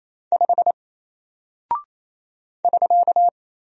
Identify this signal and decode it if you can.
Morse code, heard as a keyed tone, about 27 wpm, 700 Hz. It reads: 5 <SK>
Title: Single Character (Letter:Number:Punctuation:Prosign) Alt 28wpm